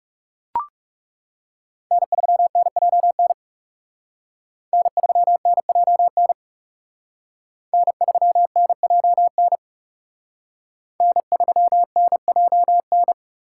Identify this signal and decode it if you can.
Morse code, heard as a keyed tone, about 38 wpm, 700 Hz. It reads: N3NJN N3NJN N3NJN N3NJN